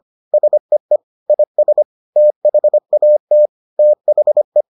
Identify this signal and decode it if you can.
Morse code, heard as a keyed tone, about 25 wpm, 600 Hz. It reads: SEE IS THAT THE